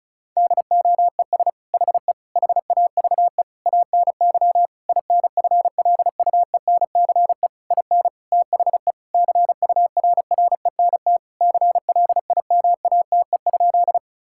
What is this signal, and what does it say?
Morse code, heard as a keyed tone, about 36 wpm, 700 Hz. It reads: DOES HE HAVE ANY INFLUENCE IN THE CURRENT CLIMATE?
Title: Sentences from Top 1600 Words 35wpm